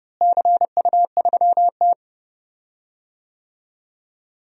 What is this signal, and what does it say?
Morse code, heard as a keyed tone, about 30 wpm, 700 Hz. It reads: CU3T
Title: DXCC-100 Call Signs - 4 Characters - Repeated 2x - 30wpm